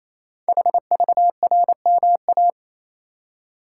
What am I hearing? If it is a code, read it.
Morse code, heard as a keyed tone, about 28 wpm, 700 Hz. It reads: HVRMA
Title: Code Groups - 5 Letters - 28wpm